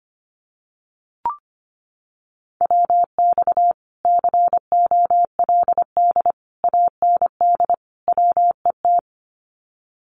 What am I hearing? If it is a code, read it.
Morse code, heard as a keyed tone, about 25 wpm, 700 Hz. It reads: WX COLD AND WET
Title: Common QSO Phrases Encore 25wpm